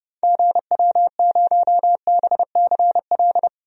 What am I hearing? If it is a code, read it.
Morse code, heard as a keyed tone, about 30 wpm, 700 Hz. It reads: GW0BCL
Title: DXCC-100 Call Signs - 6 Characters - 30wpm